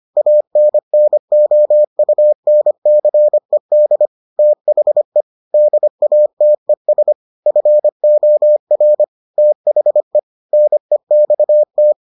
Morse code, heard as a keyed tone, about 25 wpm, 600 Hz. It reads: ANNOUNCED THE DATES FOR THE NEXT